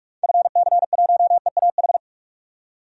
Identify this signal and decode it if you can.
Morse code, heard as a keyed tone, about 42 wpm, 700 Hz. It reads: FC1EAH
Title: DXCC-100 Call Signs - 6 Characters - 45wpm